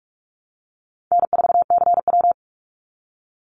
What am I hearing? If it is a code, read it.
Morse code, heard as a keyed tone, about 44 wpm, 700 Hz. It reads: N4CW
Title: Call Signs - 45wpm